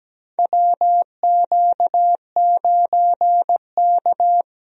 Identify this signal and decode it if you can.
Morse code, heard as a keyed tone, about 17 wpm, 700 Hz. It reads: WQ9K
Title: USA Call Signs - 4 Characters - 17wpm